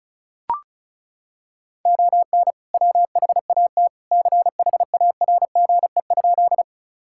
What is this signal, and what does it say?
Morse code, heard as a keyed tone, about 36 wpm, 700 Hz. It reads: ON WHAT CHARGE?
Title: Sentences from Top 1000 Words - Review Part 6 - 35wpm